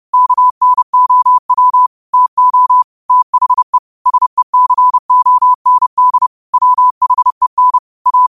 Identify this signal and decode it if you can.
Morse code, heard as a keyed tone, about 30 wpm, 1000 Hz. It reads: KNOW TO THE SECOND WHEN A
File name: qotdfiles / 30wpm